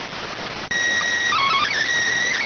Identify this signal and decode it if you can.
SSTV (slow-scan television), PD 290.